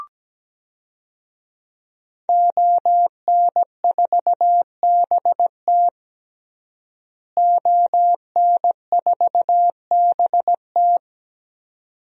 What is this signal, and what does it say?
Morse code, heard as a keyed tone, about 17 wpm, 700 Hz. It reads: ON4BT ON4BT
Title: DXCC-100 Call Signs - 5 Characters - Repeated 2x - 17wpm